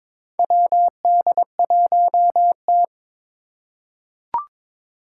Morse code, heard as a keyed tone, about 23 wpm, 700 Hz. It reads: WD1T
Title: Four Challenging Letter-Number Combinations 22wpm